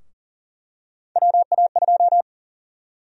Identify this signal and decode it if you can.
Morse code, heard as a keyed tone, about 41 wpm, 700 Hz. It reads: WA2
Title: USA Call Sign Prefixes - Repeated 2x - 40wpm